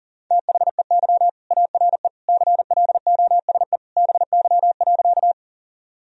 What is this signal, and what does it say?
Morse code, heard as a keyed tone, about 40 wpm, 700 Hz. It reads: THEY ARE CLOSE BY.
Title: Sentences from Top 1000 Words - Review Part 1 - 40wpm